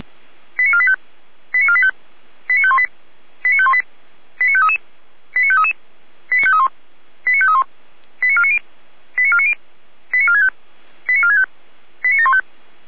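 Five-tone selective call (ZVEI), heard as quick runs of five tones, tones of about 70 ms. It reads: 89485 89485 89518 89518 89533 89533 89531 89531 89490 89490 89465 89465 89715
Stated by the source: ZVEI five-tone call received on SDR.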